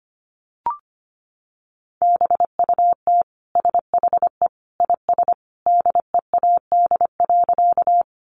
Morse code, heard as a keyed tone, about 25 wpm, 700 Hz. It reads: BUT SHE IS DEAD.
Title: Sentences from Top 1500 Words 25wpm